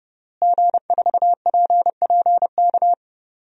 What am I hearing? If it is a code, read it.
Morse code, heard as a keyed tone, about 30 wpm, 700 Hz. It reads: G4PPK
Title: DXCC-100 Call Signs - 5 Characters - Repeated 2x - 30wpm